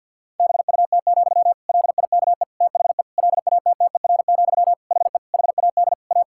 Morse code, heard as a keyed tone, about 50 wpm, 700 Hz. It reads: BUT, LIKE THE LATTER, HE HAD A
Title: Sentences from Top 1600 Words 50wpm